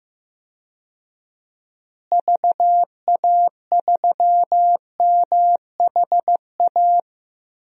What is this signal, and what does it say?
Morse code, heard as a keyed tone, about 15 wpm, 700 Hz. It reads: VA3MHA
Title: DXCC-100 Call Signs - 6 Characters - Repeated 2x - 15wpm